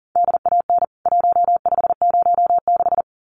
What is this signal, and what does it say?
Morse code, heard as a keyed tone, about 37 wpm, 700 Hz. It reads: DAN 1506
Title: CWT Exchange 40wpm